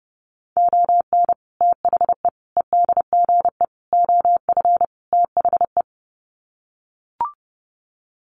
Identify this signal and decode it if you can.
Morse code, heard as a keyed tone, about 30 wpm, 700 Hz. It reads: ON THE EDGE OF THE
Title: Sets of 5 Words 30wpm